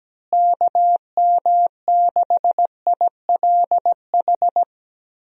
Morse code, heard as a keyed tone, about 17 wpm, 700 Hz. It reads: KM6ILH
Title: USA Call Signs - 6 Characters - Repeated 2x - 17wpm